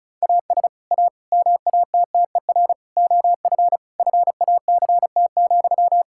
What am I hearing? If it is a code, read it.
Morse code, heard as a keyed tone, about 34 wpm, 700 Hz. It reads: AS A MATTER OF FACT,
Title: Sentences from Top 1600 Words 35wpm